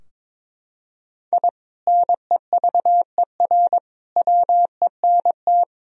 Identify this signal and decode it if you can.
Morse code, heard as a keyed tone, about 22 wpm, 700 Hz. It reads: I NEVER WENT
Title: Sentences from Top 1000 Words - Review Part 3 - 22wpm